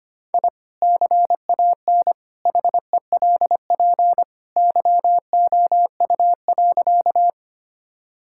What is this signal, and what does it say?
Morse code, heard as a keyed tone, about 25 wpm, 700 Hz. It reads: I CAN HELP YOU.
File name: practice / 25wpm / Sentences with 2-Word Phrases - 25wpm